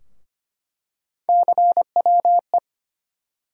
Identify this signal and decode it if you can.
Morse code, heard as a keyed tone, about 25 wpm, 700 Hz. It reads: CWE
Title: USA Call Sign Postfixes - 3 Letters 25wpm